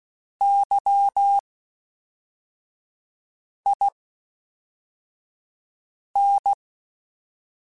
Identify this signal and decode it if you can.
Morse code, heard as a keyed tone, about 16 wpm, spaced as for 4 wpm, 800 Hz. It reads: YIN
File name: Misc / Misc-mp3s / WordPairs sp @5